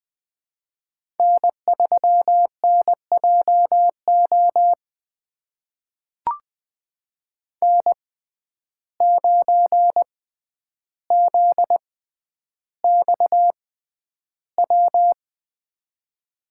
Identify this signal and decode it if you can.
Morse code, heard as a keyed tone, about 20 wpm, spaced as for 10 wpm, 700 Hz. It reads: N3NJO N9ZXW
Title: USA Call Signs - 5 Characters - 6x Character Spacing - 20wpm